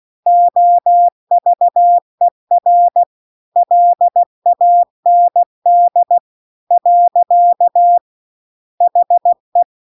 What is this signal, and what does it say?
Morse code, heard as a keyed tone, about 16 wpm, 700 Hz. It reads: OVER LAND . HE